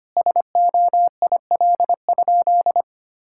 Morse code, heard as a keyed tone, about 25 wpm, 700 Hz. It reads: SOIL?
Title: Sentences from Top 1600 Words 25wpm